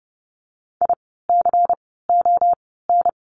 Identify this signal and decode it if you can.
Morse code, heard as a keyed tone, about 30 wpm, 700 Hz. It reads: I C O N
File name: Lesson 16f - Letter c - Words 30wpm